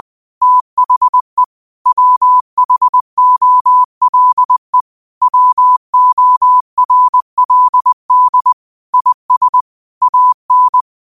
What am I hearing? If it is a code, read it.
Morse code, heard as a keyed tone, about 20 wpm, 1000 Hz. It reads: THE WHOLE WORLD IS AN
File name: qotdfiles / 20wpm